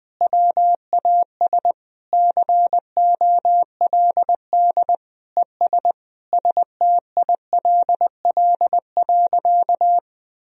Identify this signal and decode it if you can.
Morse code, heard as a keyed tone, about 20 wpm, 700 Hz. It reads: WAS COLD ES STILL.
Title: Sentences from Top 400 Words and Abbreviations 20wpm